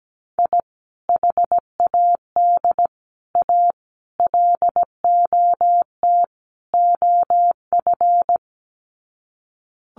Morse code, heard as a keyed tone, about 17 wpm, 700 Hz. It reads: I HAD A LOT OF
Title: Sets of 5 Words Encore 17wpm